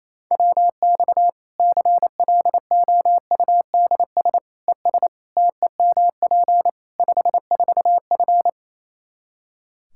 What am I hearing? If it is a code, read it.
Morse code, heard as a keyed tone, about 28 wpm, 700 Hz. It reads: WX CLOUDS ES TEMP 54F
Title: Practice QSO - Exchange 2 - Info - 28wpm